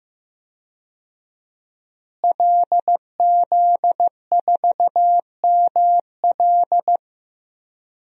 Morse code, heard as a keyed tone, about 15 wpm, 700 Hz. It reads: LZ4ML
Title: DXCC-100 Call Signs - 5 Characters - Repeated 2x - 15wpm